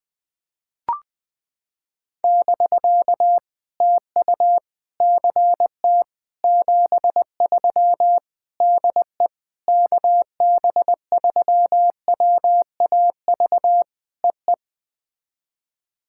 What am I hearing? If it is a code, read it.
Morse code, heard as a keyed tone, about 19 wpm, 700 Hz. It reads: <BK> TU CT 73 DE KB3WAV EE
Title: POTA QSO Practice - 20wpm